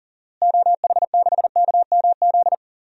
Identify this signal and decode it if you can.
Morse code, heard as a keyed tone, about 40 wpm, 700 Hz. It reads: OH6KMZ
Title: DXCC-100 Call Signs - 6 Characters - Repeated 3x - 40wpm